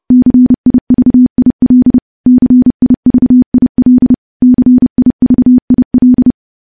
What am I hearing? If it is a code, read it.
Morse code, heard as a keyed tone, about 30 wpm, 250 Hz. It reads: CIVIL CIVIL CIVIL
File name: morse-training-30WPM